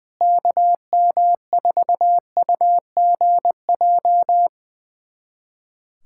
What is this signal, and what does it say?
Morse code, heard as a keyed tone, about 20 wpm, 700 Hz. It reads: KM4UGJ
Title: USA Callsign DE Callsign - 6 Characters - 20wpm